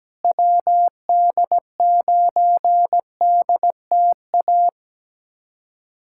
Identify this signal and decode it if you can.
Morse code, heard as a keyed tone, about 17 wpm, 700 Hz. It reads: WD9DTA
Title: USA Callsign DE Callsign - 6 Characters - 17wpm